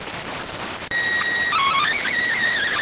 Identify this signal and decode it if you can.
SSTV (slow-scan television), PD 290.